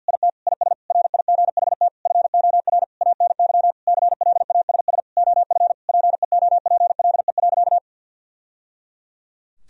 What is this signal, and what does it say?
Morse code, heard as a keyed tone, about 52 wpm, 700 Hz. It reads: IT IS RIGHT FOR ANY CLASS OF PEOPLE.
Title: Sentences from Top 1000 Words - Review Part 2 - 50wpm